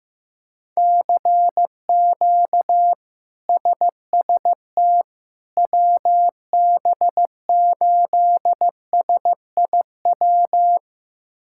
Morse code, heard as a keyed tone, about 15 wpm, 700 Hz. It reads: CQ SST WB8SIW
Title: K1USN SST QSO Practice - 15wpm